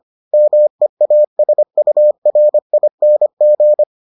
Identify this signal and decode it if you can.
Morse code, heard as a keyed tone, about 25 wpm, 600 Hz. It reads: MEASURING